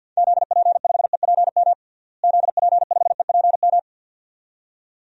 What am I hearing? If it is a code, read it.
Morse code, heard as a keyed tone, about 52 wpm, 700 Hz. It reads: ZP5EPM ZP5EPM
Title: DXCC-100 Call Signs - 6 Characters - Repeated 2x - 50wpm